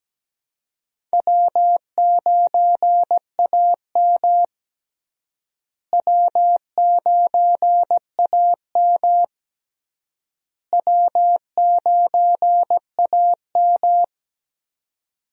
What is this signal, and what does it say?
Morse code, heard as a keyed tone, about 17 wpm, 700 Hz. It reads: W9AM W9AM W9AM
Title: USA Call Signs - 4 Characters - Repeated 3x - 17wpm